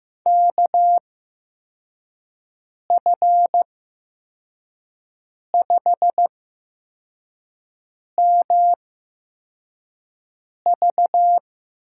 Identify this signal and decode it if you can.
Morse code, heard as a keyed tone, about 15 wpm, spaced as for 4 wpm, 700 Hz. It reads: KF5MV
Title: USA Call Signs - 6 Characters - 8x Character Spacing - 15wpm